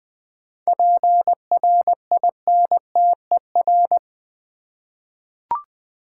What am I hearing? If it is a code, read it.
Morse code, heard as a keyed tone, about 20 wpm, 700 Hz. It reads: PRINTER
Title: 7 Letter Words - First 3 Letters Given - 20wpm